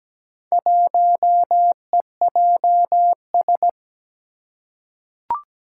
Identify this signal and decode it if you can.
Morse code, heard as a keyed tone, about 17 wpm, 700 Hz. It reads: 1EJS
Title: Four Challenging Letter-Number Combinations 17wpm